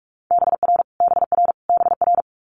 Morse code, heard as a keyed tone, about 46 wpm, 700 Hz. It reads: BR BR BR
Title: Multi-letter Phonemes Repeated 3x 45wpm